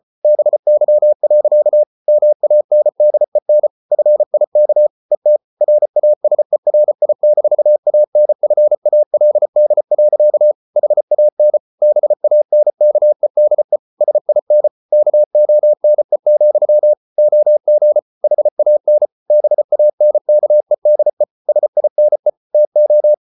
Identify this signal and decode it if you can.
Morse code, heard as a keyed tone, about 34 wpm, 600 Hz. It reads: BY. MANDEN FIK ET RASERI-ANFALD. HAN BANKEDE SIN KONE, OG HAN BANKEDE SINE TO
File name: Ferie QRQ 34wpm